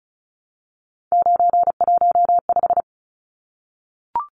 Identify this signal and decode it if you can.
Morse code, heard as a keyed tone, about 34 wpm, 700 Hz. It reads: 915
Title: Numbers - 3-Digit 35wpm